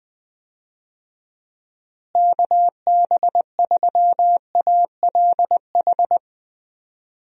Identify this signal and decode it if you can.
Morse code, heard as a keyed tone, about 20 wpm, 700 Hz. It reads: KB3ALH
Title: USA Call Signs - 6 Characters - Repeated 3x - 20wpm